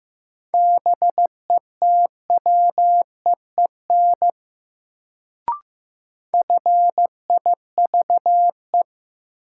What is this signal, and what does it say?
Morse code, heard as a keyed tone, about 15 wpm, 700 Hz. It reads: BETWEEN FIVE
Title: Sentences from Top 400 Words - New Word Drill - 15wpm